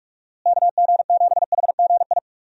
Morse code, heard as a keyed tone, about 46 wpm, 700 Hz. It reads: KG7HGI
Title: USA Call Signs - 6 Characters - Repeated 2x - 45wpm